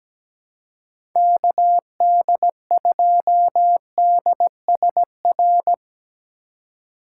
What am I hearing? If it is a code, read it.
Morse code, heard as a keyed tone, about 17 wpm, 700 Hz. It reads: KD2DSR
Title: USA Call Signs - 6 Characters - 3x Character Spacing - 17wpm